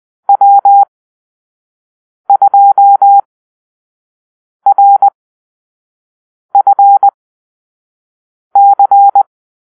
Morse code, heard as a keyed tone, about 20 wpm, spaced as for 5 wpm, 800 Hz. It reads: W2RFC
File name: arrl / Call-Sign Drill-4